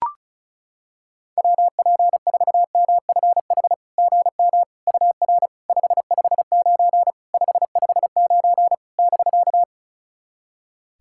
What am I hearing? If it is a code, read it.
Morse code, heard as a keyed tone, about 36 wpm, 700 Hz. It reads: WP4MFH GM UR 559 559 <BK>
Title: SOTA QSO Practice 35wpm